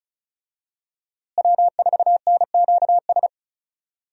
Morse code, heard as a keyed tone, about 34 wpm, 700 Hz. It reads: W4NQS
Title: USA Call Signs - 5 Characters - Repeated 2x - 35wpm